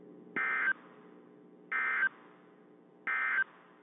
Emergency Alert System (SAME) tones, heard as harsh buzzing bursts. It is heard from 0.3 to 3.4 s.
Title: WZ2523 - NOAA Weather Radio